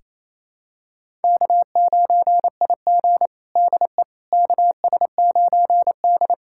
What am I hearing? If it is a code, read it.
Morse code, heard as a keyed tone, about 28 wpm, 700 Hz. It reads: K9IG DE KS9D
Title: USA Callsign DE Callsign - 4 Characters - 28wpm